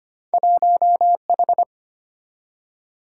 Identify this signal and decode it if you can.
Morse code, heard as a keyed tone, about 32 wpm, 700 Hz. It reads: 1H